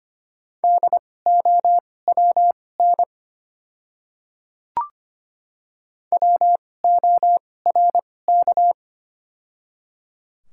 Morse code, heard as a keyed tone, about 25 wpm, spaced as for 18 wpm, 690 Hz. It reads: DOWN WORK
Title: 2x Character Spacing - Top 200 Words - 25wpm